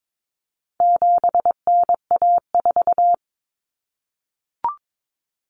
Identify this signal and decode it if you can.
Morse code, heard as a keyed tone, about 22 wpm, 700 Hz. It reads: 7NA4
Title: Common Cut Numbers - 4-Digit 22wpm